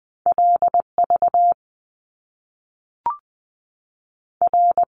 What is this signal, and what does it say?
Morse code, heard as a keyed tone, about 20 wpm, 700 Hz. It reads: LV R
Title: Top Two Letter Combinations in Words 20wpm